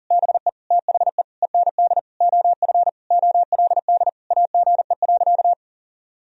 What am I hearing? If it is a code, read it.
Morse code, heard as a keyed tone, about 37 wpm, 700 Hz. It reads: BE THE END OF OLD AGE.
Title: Sentences from Top 1000 Words - Review Part 3 - 40wpm